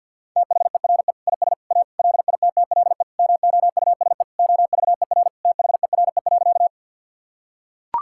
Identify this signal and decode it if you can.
Morse code, heard as a keyed tone, about 45 wpm, 700 Hz. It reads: THERE IS A LITTLE MOUSE OVER THERE.
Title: Sentences from Top 1000 Words - Review Part 5 - 50wpm